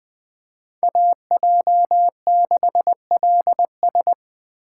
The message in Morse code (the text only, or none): AJ6LS